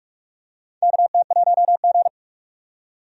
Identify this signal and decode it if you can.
Morse code, heard as a keyed tone, about 44 wpm, 700 Hz. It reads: KT1G
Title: USA Call Signs - 4 Characters - 45wpm